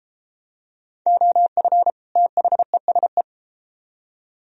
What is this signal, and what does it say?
Morse code, heard as a keyed tone, about 33 wpm, 700 Hz. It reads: OF THESE